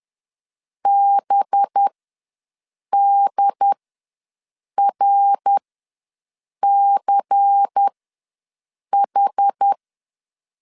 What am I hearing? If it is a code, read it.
Morse code, heard as a keyed tone, about 10 wpm, spaced as for 6 wpm, 800 Hz. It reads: BDRCH
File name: LEZIONI / 7 GF9Z / 7 G ws 10wpm 3843